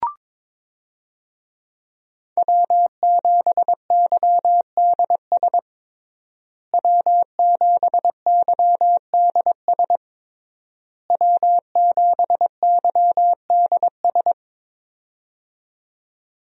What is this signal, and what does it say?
Morse code, heard as a keyed tone, about 22 wpm, 700 Hz. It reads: W7YDS W7YDS W7YDS
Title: USA Call Signs - 5 Characters - Repeated 3x - 22wpm